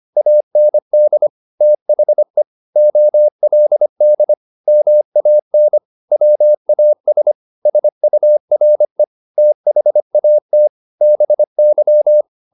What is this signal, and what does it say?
Morse code, heard as a keyed tone, about 25 wpm, 600 Hz. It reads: AND THE OLD MAN WAS SURE THAT BY